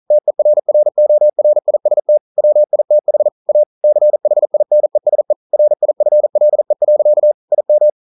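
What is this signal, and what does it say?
Morse code, heard as a keyed tone, about 41 wpm, 600 Hz. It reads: TERRORIST WITH A CHINESE RIFLE. IM
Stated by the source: Morse code practice at 41 wpm, from CW med Gnister.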